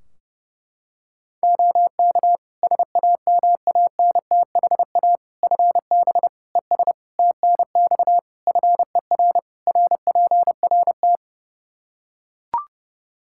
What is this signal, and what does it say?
Morse code, heard as a keyed tone, about 30 wpm, 700 Hz. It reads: OK SAMANTHA FB ES TNX FER RPRT
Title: Practice QSO - Exchange 2 - Info - 30wpm